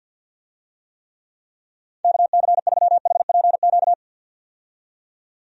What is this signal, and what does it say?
Morse code, heard as a keyed tone, about 52 wpm, 700 Hz. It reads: KC3HPQ
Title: USA Call Signs - 6 Characters - 50wpm